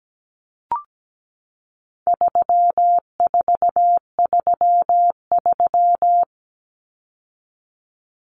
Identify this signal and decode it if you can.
Morse code, heard as a keyed tone, about 17 wpm, 700 Hz. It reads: 3433
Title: Numbers - 4-Digit 17wpm